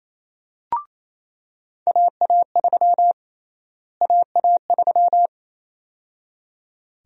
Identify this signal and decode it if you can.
Morse code, heard as a keyed tone, about 28 wpm, 700 Hz. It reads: AA3 AA3
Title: USA Call Sign Prefixes - Repeated 2x - 28wpm